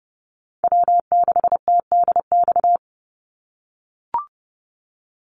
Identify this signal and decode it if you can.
Morse code, heard as a keyed tone, about 30 wpm, 700 Hz. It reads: W6TDX
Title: NA SOTA Chaser Callsigns 30wpm